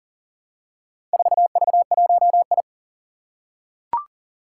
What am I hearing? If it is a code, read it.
Morse code, heard as a keyed tone, about 40 wpm, 700 Hz. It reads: 4V1I
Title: Four Challenging Letter-Number Combinations 40wpm